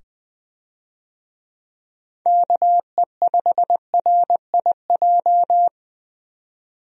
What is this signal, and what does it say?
Morse code, heard as a keyed tone, about 20 wpm, 700 Hz. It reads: KE5RIJ
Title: USA Call Signs - 6 Characters - 20wpm